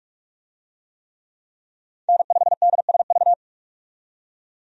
Morse code, heard as a keyed tone, about 46 wpm, 700 Hz. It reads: N5DSV
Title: USA Call Signs - 5 Characters - 45wpm